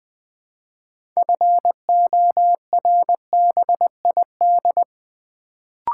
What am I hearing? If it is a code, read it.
Morse code, heard as a keyed tone, about 20 wpm, 690 Hz. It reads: FORBID
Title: Build A Word - 2 Letters At A Time - 6-Letter Words - 20wpm